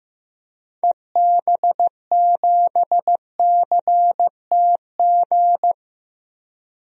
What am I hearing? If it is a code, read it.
Morse code, heard as a keyed tone, about 15 wpm, 700 Hz. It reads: EB7CTG
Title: DXCC-100 Call Signs - 6 Characters - 15wpm